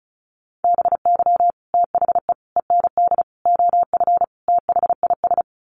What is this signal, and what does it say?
Morse code, heard as a keyed tone, about 34 wpm, 700 Hz. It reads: BY THE END OF THIS
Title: Sets of 5 Words Encore 35wpm